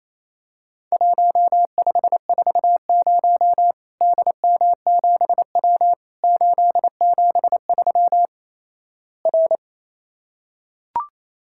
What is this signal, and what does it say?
Morse code, heard as a keyed tone, about 28 wpm, 700 Hz. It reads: 1540 DM7W 873 R
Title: Worked All Europe DX Contest - QTC Exchange - 28wpm